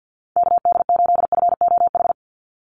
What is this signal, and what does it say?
Morse code, heard as a keyed tone, about 50 wpm, 700 Hz. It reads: KD8FOH
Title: USA Call Signs 50wpm